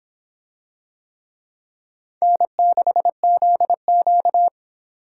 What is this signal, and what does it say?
Morse code, heard as a keyed tone, about 26 wpm, 700 Hz. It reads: N6ZQ